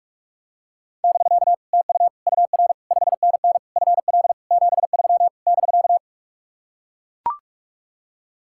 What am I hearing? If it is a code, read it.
Morse code, heard as a keyed tone, about 44 wpm, 700 Hz. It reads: <BK> TU UR 5NN FL 73 <BK>